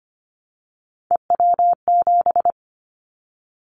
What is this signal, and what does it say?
Morse code, heard as a keyed tone, about 25 wpm, 700 Hz. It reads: EW7
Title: Callsign Prefixes for Top 50 DXCC - 25wpm